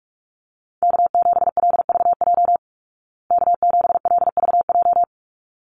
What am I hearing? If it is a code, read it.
Morse code, heard as a keyed tone, about 44 wpm, 700 Hz. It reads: K7LVJ K7LVJ
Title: Call Signs Repeated 3x - 45wpm